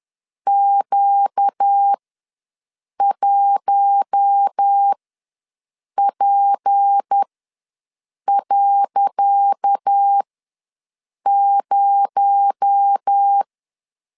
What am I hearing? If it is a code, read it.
Morse code, heard as a keyed tone, about 10 wpm, spaced as for 6 wpm, 800 Hz. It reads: Q1P.0
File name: LEZIONI / 10 SIGNS / 10D COMA ws 10wpm 4314